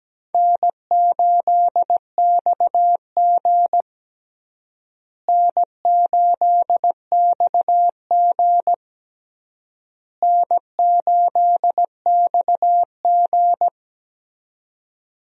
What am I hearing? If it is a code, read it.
Morse code, heard as a keyed tone, about 17 wpm, 700 Hz. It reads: N8XG N8XG N8XG
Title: USA Call Signs - 4 Characters - Repeated 3x - 17wpm